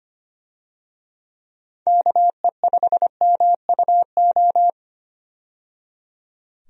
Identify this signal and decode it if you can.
Morse code, heard as a keyed tone, about 25 wpm, 700 Hz. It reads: KE5MUO
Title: USA Call Signs - 6 Characters - 25wpm